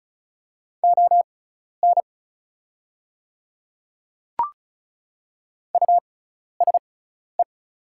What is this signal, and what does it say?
Morse code, heard as a keyed tone, about 34 wpm, spaced as for 12 wpm, 700 Hz. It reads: ON USE
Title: 6x Character Spacing - Top 100 Words - 35wpm